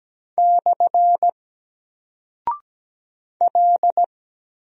Morse code, heard as a keyed tone, about 17 wpm, 700 Hz. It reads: / L
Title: Single Character (Letter:Number:Punctuation:Prosign) 17wpm